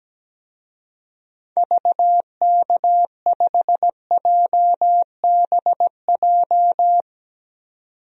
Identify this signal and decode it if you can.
Morse code, heard as a keyed tone, about 17 wpm, 700 Hz. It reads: VK5JBJ
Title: DXCC-100 Call Signs - 6 Characters - 17wpm